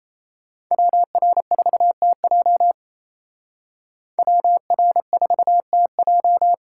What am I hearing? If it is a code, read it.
Morse code, heard as a keyed tone, about 33 wpm, 700 Hz. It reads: WR4TJ WR4TJ
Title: Speed Racing - USA 2x2 Call Signs 28wpm